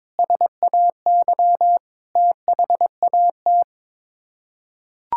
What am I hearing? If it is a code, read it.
Morse code, heard as a keyed tone, about 22 wpm, 700 Hz. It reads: SAY THAT
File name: practice / 22wpm / Sets of 8 Words - 22wpm